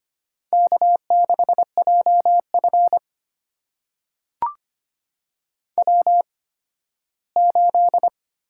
Morse code, heard as a keyed tone, about 25 wpm, spaced as for 15 wpm, 700 Hz. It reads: K6JF W8
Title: USA Call Signs - 4 Characters - 8x Character Spacing - 25wpm